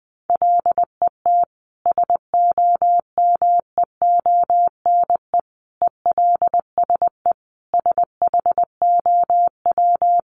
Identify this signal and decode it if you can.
Morse code, heard as a keyed tone, about 20 wpm, 700 Hz. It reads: LET SOMEONE ELSE SHOW
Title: Sentences from Top 1200 Words 20wpm